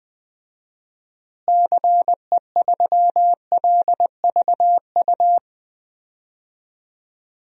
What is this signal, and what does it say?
Morse code, heard as a keyed tone, about 21 wpm, 700 Hz. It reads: CE3LVU
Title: DXCC-100 Call Signs - 6 Characters - Repeated 2x - 20wpm